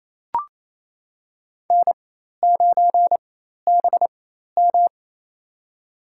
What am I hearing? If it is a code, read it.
Morse code, heard as a keyed tone, about 28 wpm, spaced as for 13 wpm, 700 Hz. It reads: N9BM
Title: USA Call Signs - 4 Characters - 4x Character Spacing - 28wpm